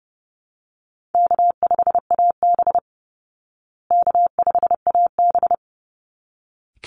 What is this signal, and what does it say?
Morse code, heard as a keyed tone, about 30 wpm, 700 Hz. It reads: K5AB K5AB
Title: Easy Call Signs Repeated 2x - 30wpm